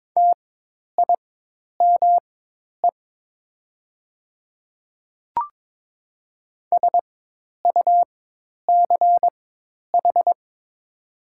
Morse code, heard as a keyed tone, about 22 wpm, spaced as for 10 wpm, 700 Hz. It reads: TIME SUCH
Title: 4x Character Spacing - Top 200 Words - 22wpm